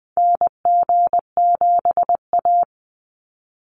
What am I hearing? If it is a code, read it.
Morse code, heard as a keyed tone, about 19 wpm, 700 Hz. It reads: NG7A
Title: NA SOTA Activator Callsigns 20wpm